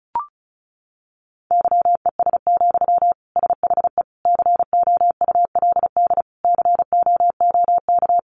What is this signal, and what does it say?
Morse code, heard as a keyed tone, about 34 wpm, 700 Hz. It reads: YES, SHE COULD COOK
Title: Sentences from Top 700 Words 35wpm